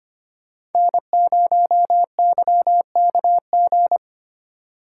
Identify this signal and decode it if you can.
Morse code, heard as a keyed tone, about 32 wpm, 700 Hz. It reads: N0YKG